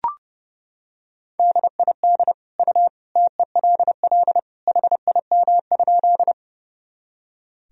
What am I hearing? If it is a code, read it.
Morse code, heard as a keyed tone, about 32 wpm, 700 Hz. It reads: DID U TELL HIM?
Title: Sentences from Top 200 Words and Abbreviations 30wpm